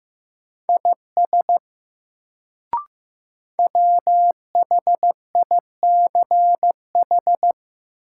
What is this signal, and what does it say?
Morse code, heard as a keyed tone, about 16 wpm, 700 Hz. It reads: IS WHICH
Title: Sentences from Top 100 Words - New Word Drill - 15wpm